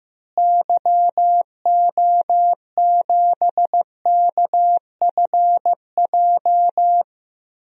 Morse code, heard as a keyed tone, about 15 wpm, 700 Hz. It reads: YO7KFJ
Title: DXCC-100 Call Signs - 6 Characters - Repeated 2x - 15wpm